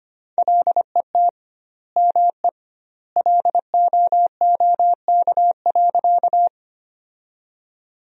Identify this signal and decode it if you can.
Morse code, heard as a keyed tone, about 25 wpm, 700 Hz. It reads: LET ME LOOK.
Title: 2x Word-Spacing - Sentences from Top 300 Words - 25wpm